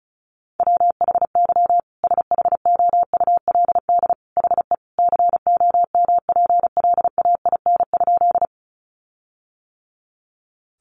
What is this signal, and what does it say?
Morse code, heard as a keyed tone, about 34 wpm, 700 Hz. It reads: WHY SHOULD HE COMPLAIN?
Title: Sentences from Top 1300 Words 35wpm